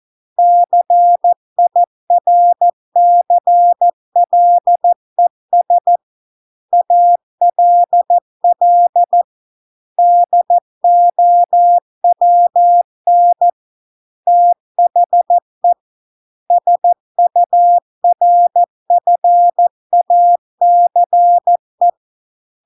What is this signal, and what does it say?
Morse code, heard as a keyed tone, about 14 wpm, 700 Hz. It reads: CIRCLES ALL DOWN THE SURFACE